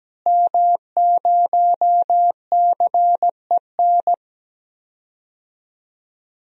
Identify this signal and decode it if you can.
Morse code, heard as a keyed tone, about 17 wpm, 700 Hz. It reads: M0CEN
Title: DXCC-100 Call Signs - 5 Characters - Repeated 2x - 17wpm